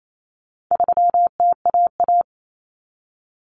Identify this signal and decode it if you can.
Morse code, heard as a keyed tone, about 28 wpm, 700 Hz. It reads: 3TAA
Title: Common Cut Numbers - 4-Digit 28wpm